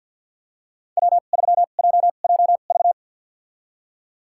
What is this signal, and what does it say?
Morse code, heard as a keyed tone, about 51 wpm, spaced as for 36 wpm, 700 Hz. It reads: W3JJV
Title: USA Call Signs - 5 Characters - 2x Character Spacing - 50wpm